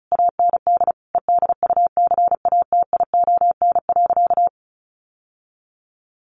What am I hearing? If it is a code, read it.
Morse code, heard as a keyed tone, about 34 wpm, 700 Hz. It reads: AND EDUCATION.
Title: Sentences from Top 1500 Words 35wpm